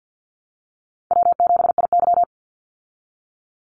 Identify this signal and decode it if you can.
Morse code, heard as a keyed tone, about 50 wpm, 700 Hz. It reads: W7IY
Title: Easy Call Signs Repeated 3x - 50wpm